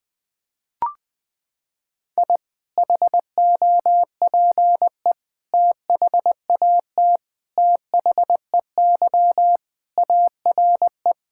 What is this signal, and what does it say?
Morse code, heard as a keyed tone, about 20 wpm, 700 Hz. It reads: I HOPE THAT THEY ARE
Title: Sentences with 3-Word Phrases - 20wpm